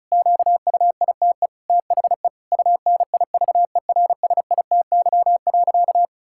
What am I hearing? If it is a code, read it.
Morse code, heard as a keyed tone, about 36 wpm, 700 Hz. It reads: QUITE THE UNIVERSITY.
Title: Sentences from Top 1000 Words - Review Part 5 - 35wpm